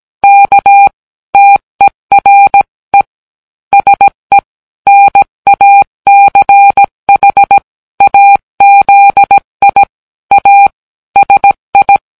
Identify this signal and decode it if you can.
Morse code, heard as a keyed tone, about 17 wpm, 800 Hz. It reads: KTERE SE NACHAZI A SI